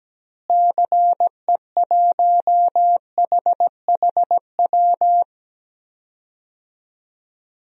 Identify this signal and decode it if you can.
Morse code, heard as a keyed tone, about 17 wpm, 700 Hz. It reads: CE1HHW
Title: DXCC-100 Call Signs - 6 Characters - Repeated 3x - 17wpm